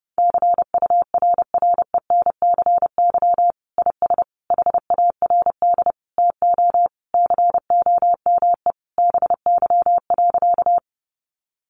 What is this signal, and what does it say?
Morse code, heard as a keyed tone, about 30 wpm, 700 Hz. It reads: CURRENCY IS HARD TO COME BY.
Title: Sentences from Top 1500 Words 30wpm